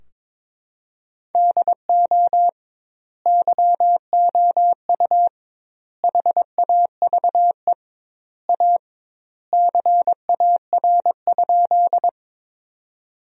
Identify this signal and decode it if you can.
Morse code, heard as a keyed tone, about 22 wpm, 700 Hz. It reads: DO YOU HAVE A CAR?
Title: 2x Word-Spacing - Sentences from Top 300 Words - 22wpm